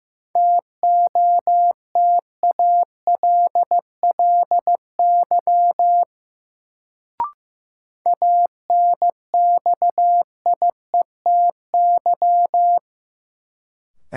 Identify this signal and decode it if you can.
Morse code, heard as a keyed tone, about 15 wpm, 700 Hz. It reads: TOTALLY ANXIETY
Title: Sentences from Top 1600 Words - New Word Drill Alt - 15wpm